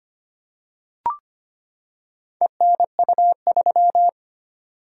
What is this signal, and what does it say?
Morse code, heard as a keyed tone, about 25 wpm, 700 Hz. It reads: ENU3
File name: practice / 25wpm / Four Letter-Number Combinations - 25wpm